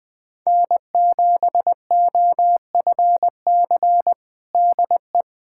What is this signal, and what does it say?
Morse code, heard as a keyed tone, about 20 wpm, 700 Hz. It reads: N7OFC DE
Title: USA Callsign DE Callsign - 5 Characters - 20wpm